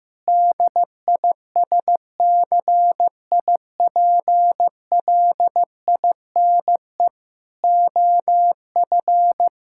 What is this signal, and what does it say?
Morse code, heard as a keyed tone, about 15 wpm, 700 Hz. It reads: DISCIPLINE OF
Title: Sentences from Top 1000 Words - Review Part 6 - 15wpm